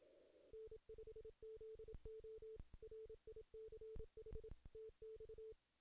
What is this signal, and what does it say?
Morse code, heard as a keyed tone, about 26 wpm, 440 Hz. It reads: N5ZO RICH TX
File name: 2016 JAN CW NAQP / 2016 Jan naqp k5ot - 0-11